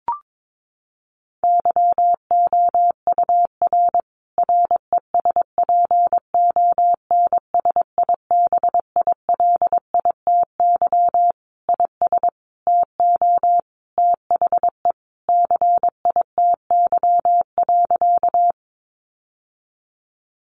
Morse code, heard as a keyed tone, about 22 wpm, 700 Hz. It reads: YOUR RESPONSIBILITY IS TO THE CITY.
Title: Sentences from Top 1200 Words 22wpm